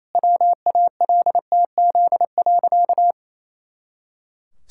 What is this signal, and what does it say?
Morse code, heard as a keyed tone, about 28 wpm, 700 Hz. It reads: WALTZ.
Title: Pangram Challenge v2 - 28wpm